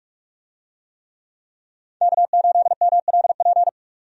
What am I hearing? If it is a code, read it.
Morse code, heard as a keyed tone, about 41 wpm, 700 Hz. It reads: K8MLP
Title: USA Call Signs - 5 Characters - 45wpm